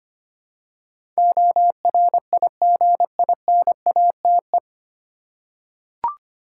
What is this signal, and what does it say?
Morse code, heard as a keyed tone, about 25 wpm, 700 Hz. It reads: ORIGINATE
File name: practice / 25wpm / 9 Letter Words - First 3 Letters Given - 25wpm